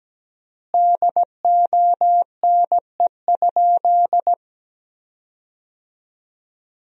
Morse code, heard as a keyed tone, about 17 wpm, 700 Hz. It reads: DONE?
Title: 3x Word-Spacing - Sentences from Top 500 Words - 17wpm